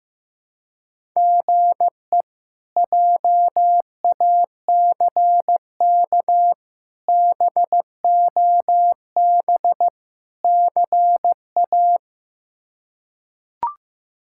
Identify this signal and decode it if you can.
Morse code, heard as a keyed tone, about 15 wpm, 700 Hz. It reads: GE JACK BOB CA
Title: K1USN SST QSO Practice - 15wpm